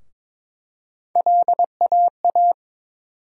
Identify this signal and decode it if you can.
Morse code, heard as a keyed tone, about 22 wpm, 700 Hz. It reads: LAA
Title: USA Call Sign Postfixes - 3 Letters 22wpm